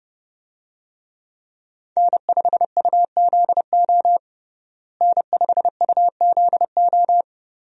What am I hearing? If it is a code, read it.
Morse code, heard as a keyed tone, about 30 wpm, 700 Hz. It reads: N5UZO N5UZO
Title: USA Call Signs - 5 Characters - Repeated 2x - 30wpm